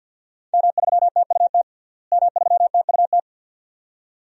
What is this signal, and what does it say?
Morse code, heard as a keyed tone, about 49 wpm, 700 Hz. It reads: M3TUT M3TUT
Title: DXCC-100 Call Signs - 5 Characters - Repeated 2x - 50wpm